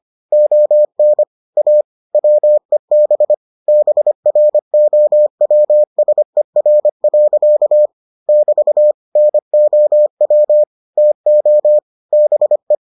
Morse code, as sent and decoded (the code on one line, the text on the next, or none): --- -. / .- / .-- . -... / -... .-. --- .-- ... .-. .-.-.- / -...- / -. --- .-- / - --- / -... .
ON A WEB BROWSER. <BT> NOW TO BE